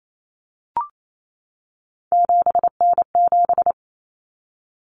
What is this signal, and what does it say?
Morse code, heard as a keyed tone, about 28 wpm, 700 Hz. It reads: 7N7
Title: Common Cut Numbers - 3-Digit 28wpm